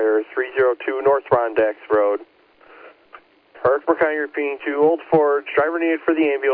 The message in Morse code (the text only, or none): none